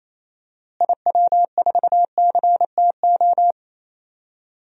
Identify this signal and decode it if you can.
Morse code, heard as a keyed tone, about 28 wpm, 700 Hz. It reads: IW4CTO
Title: DXCC-100 Call Signs - 6 Characters - Repeated 3x - 28wpm